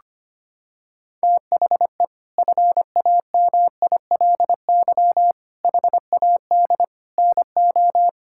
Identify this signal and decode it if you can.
Morse code, heard as a keyed tone, about 25 wpm, 700 Hz. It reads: THE FAMILY HAD NO
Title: Sentences from Top 1600 Words 25wpm